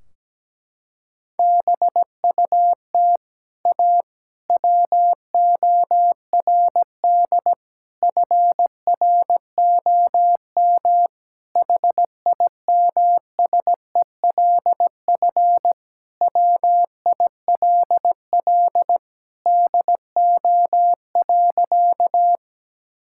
Morse code, heard as a keyed tone, about 17 wpm, 700 Hz. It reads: BUT A WORD FROM HIMSELF WILL DO.
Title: Sentences from Top 1600 Words 17wpm